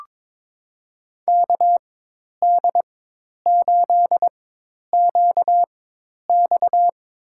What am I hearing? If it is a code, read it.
Morse code, heard as a keyed tone, about 22 wpm, spaced as for 10 wpm, 700 Hz. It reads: KD8QX
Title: USA Call Signs - 4x Character Spacing - 22wpm